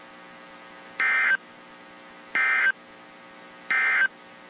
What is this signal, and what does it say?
Emergency Alert System (SAME) tones, heard as harsh buzzing bursts. It is heard from 0.9 to 4.2 s.
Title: WZ2523 - NOAA Weather Radio